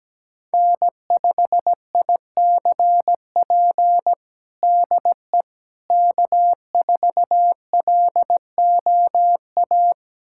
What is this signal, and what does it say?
Morse code, heard as a keyed tone, about 17 wpm, 700 Hz. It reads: N5ICP DE K4LOA
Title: USA Callsign DE Callsign - 5 Characters - 17wpm